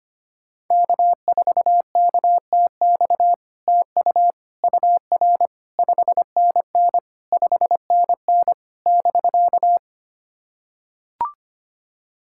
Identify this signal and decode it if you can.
Morse code, heard as a keyed tone, about 25 wpm, 700 Hz. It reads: K4KTX TU UR 5NN 5NN <BK>
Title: SOTA QSO Practice 25wpm